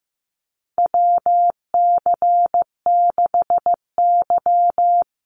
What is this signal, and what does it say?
Morse code, heard as a keyed tone, about 15 wpm, 700 Hz. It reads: WC6Y
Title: Call Signs Repeated 2x - 15wpm